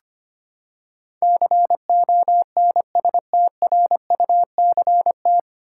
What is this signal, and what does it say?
Morse code, heard as a keyed tone, about 25 wpm, 700 Hz. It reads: CONSTRUCT
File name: Nine Letter Words - 25wpm